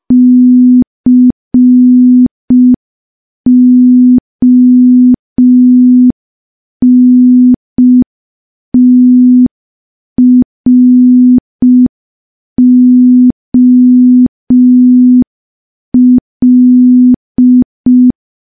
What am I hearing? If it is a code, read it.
Morse code, heard as a keyed tone, about 5 wpm, 250 Hz. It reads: CONTROL
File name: morse / morse-training-05WPM